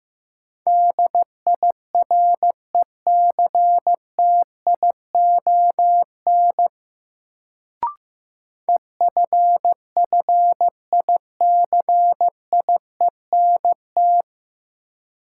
Morse code, heard as a keyed tone, about 15 wpm, 700 Hz. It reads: DIRECTION EFFICIENT
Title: Sentences from Top 1300 Words - New Word Drill Alt - 15wpm